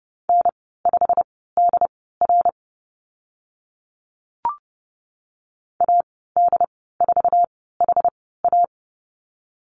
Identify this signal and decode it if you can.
Morse code, heard as a keyed tone, about 30 wpm, spaced as for 17 wpm, 700 Hz. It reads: N5DR AD4HA
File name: Lesson 15g - Number 5 - Call Signs 30wpm